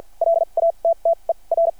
Morse code, heard as a keyed tone, about 52 wpm, spaced as for 36 wpm, 660 Hz. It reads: LATTER
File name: archived / CRSnet